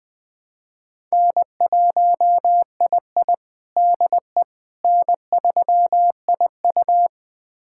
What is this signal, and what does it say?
Morse code, heard as a keyed tone, about 21 wpm, 700 Hz. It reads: N1II DE N3IU
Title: USA Callsign DE Callsign - 4 Characters - 20wpm